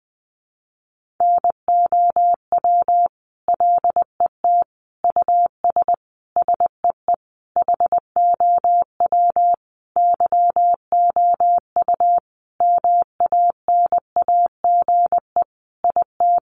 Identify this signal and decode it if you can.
Morse code, heard as a keyed tone, about 20 wpm, 700 Hz. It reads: NOW LET US SEE HOW YOU MANAGE IT
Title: Sentences from Top 1400 Words 20wpm